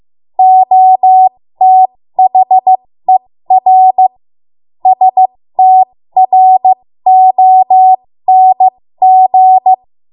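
Morse code, heard as a keyed tone, about 15 wpm, spaced as for 13 wpm, 750 Hz. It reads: OTHER STRONG
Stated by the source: 13 wpm Morse code practice archive transmission.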